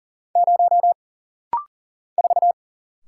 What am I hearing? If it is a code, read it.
Morse code, heard as a keyed tone, about 40 wpm, 700 Hz. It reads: 0 4
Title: Single Letter-Number - Rapid-Fire 40wpm